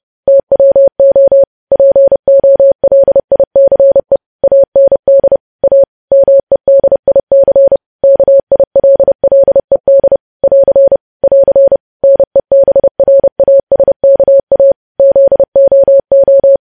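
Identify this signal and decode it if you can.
Morse code, heard as a keyed tone, about 30 wpm, 550 Hz. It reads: TWO POLICE AND A MEDIC KILLED <AR> <AR> NEBRASKA ZOO
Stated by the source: News headlines sent in Morse at 30.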